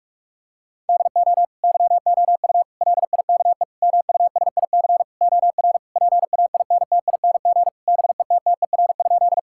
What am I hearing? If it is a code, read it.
Morse code, heard as a keyed tone, about 46 wpm, 700 Hz. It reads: DO YOU LIKE MUSIC OR PAINTING BETTER?
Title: Sentences from Top 1000 Words - Review Part 6 - 45wpm